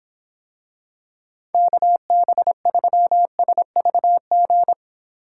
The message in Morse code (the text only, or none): KB3SVG